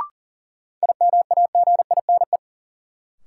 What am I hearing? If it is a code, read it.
Morse code, heard as a keyed tone, about 40 wpm, 700 Hz. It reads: IMAGINE
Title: Sentences from Top 1400 Words - New Word Drill - 40wpm